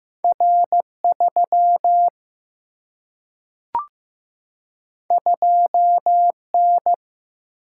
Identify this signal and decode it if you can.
Morse code, heard as a keyed tone, about 15 wpm, 700 Hz. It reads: R 3 2 N
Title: Single-Letter Single-Number 15wpm